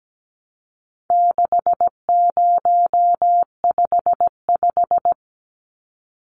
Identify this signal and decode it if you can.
Morse code, heard as a keyed tone, about 17 wpm, 700 Hz. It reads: 6055
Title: Numbers - 4-Digit 17wpm